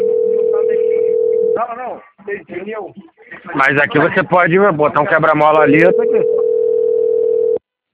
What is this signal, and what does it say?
Ringback tone (North America), 440 + 480 Hz, with speech.
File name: rg-1600-unknown- - - .79852